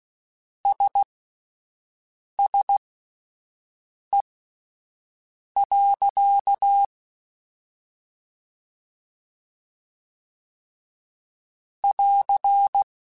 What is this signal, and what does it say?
Morse code, heard as a keyed tone, about 16 wpm, spaced as for 5 wpm, 800 Hz. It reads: SSE. <AR>
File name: Spaced Characters / Spaced Characters-mp3s / Weather 4 14 sp @10